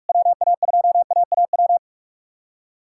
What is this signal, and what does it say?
Morse code, heard as a keyed tone, about 42 wpm, 700 Hz. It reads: WA2AAW
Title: K1USN SST QSO Practice - 45wpm